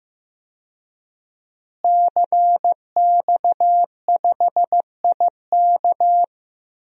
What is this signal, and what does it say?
Morse code, heard as a keyed tone, about 15 wpm, 700 Hz. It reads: CX5IK
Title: DXCC-100 Call Signs - 5 Characters - Repeated 2x - 15wpm